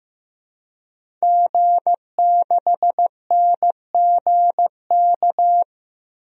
Morse code, heard as a keyed tone, about 15 wpm, 700 Hz. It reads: G6NGK
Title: DXCC-100 Call Signs - 5 Characters - Repeated 3x - 15wpm